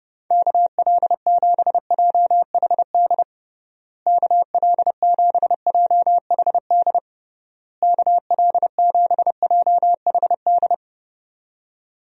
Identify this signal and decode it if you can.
Morse code, heard as a keyed tone, about 32 wpm, 700 Hz. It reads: KL7JHD KL7JHD KL7JHD
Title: USA Call Signs - 6 Characters - Repeated 3x - 30wpm